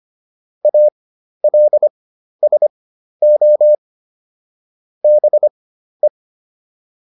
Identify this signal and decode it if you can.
Morse code, heard as a keyed tone, about 25 wpm, spaced as for 12 wpm, 600 Hz. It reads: ALSO BE